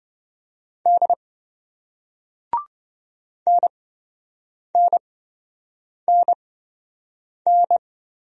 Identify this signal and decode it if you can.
Morse code, heard as a keyed tone, about 26 wpm, spaced as for 7 wpm, 700 Hz. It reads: D NNNN